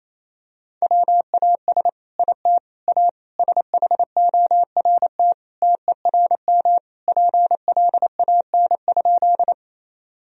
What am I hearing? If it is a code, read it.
Morse code, heard as a keyed tone, about 28 wpm, 700 Hz. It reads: WAS IT A SHORT TERM PLAN?
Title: Sentences from Top 1000 Words - Review Part 4 - 28wpm